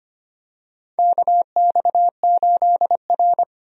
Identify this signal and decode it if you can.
Morse code, heard as a keyed tone, about 25 wpm, 700 Hz. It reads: KX8R
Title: USA Call Signs - 4 Characters - 25wpm